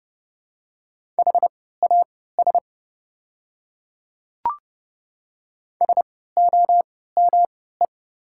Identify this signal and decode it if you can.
Morse code, heard as a keyed tone, about 30 wpm, spaced as for 17 wpm, 690 Hz. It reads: HAS SOME
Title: 3x Character Spacing - Top 200 Words - 30wpm